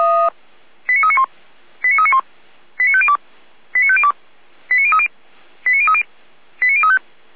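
Five-tone selective call (ZVEI), heard as quick runs of five tones, tones of about 70 ms. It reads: 89391 89391 89502 89502 89030 89030 89035
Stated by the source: ZVEI five-tone call 89042 received by SDR.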